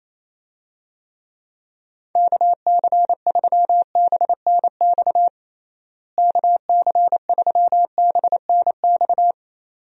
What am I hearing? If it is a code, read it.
Morse code, heard as a keyed tone, about 28 wpm, 700 Hz. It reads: KC3BNX KC3BNX
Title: USA Call Signs - 6 Characters - Repeated 2x - 28wpm